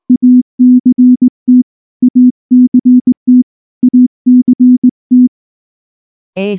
Morse code, heard as a keyed tone, about 19 wpm, 250 Hz. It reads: ACT ACT ACT
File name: morse / morse-training-20WPM-1